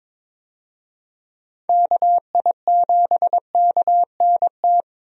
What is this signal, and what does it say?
Morse code, heard as a keyed tone, about 22 wpm, 700 Hz. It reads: KI7KNT
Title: USA Call Signs - 6 Characters - 22wpm